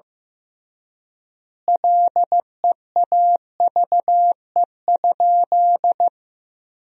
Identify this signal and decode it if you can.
Morse code, heard as a keyed tone, about 16 wpm, 700 Hz. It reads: LEAVE?
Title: 3x Word-Spacing - Sentences from Top 400 Words - 15wpm